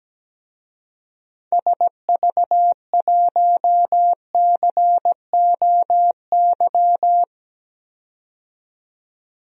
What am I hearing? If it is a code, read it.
Morse code, heard as a keyed tone, about 17 wpm, 700 Hz. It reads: SV1COY
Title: DXCC-100 Call Signs - 6 Characters - 17wpm